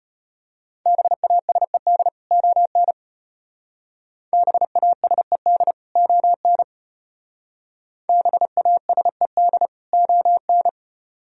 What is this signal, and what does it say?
Morse code, heard as a keyed tone, about 34 wpm, 700 Hz. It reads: BASED ON BASED ON BASED ON